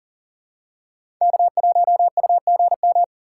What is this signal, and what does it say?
Morse code, heard as a keyed tone, about 40 wpm, 700 Hz. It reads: K1UGM
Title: USA Call Signs - 5 Characters - 4x Character Spacing - 40wpm